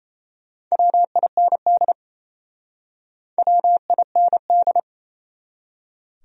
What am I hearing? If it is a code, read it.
Morse code, heard as a keyed tone, about 37 wpm, 700 Hz. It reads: WIND WIND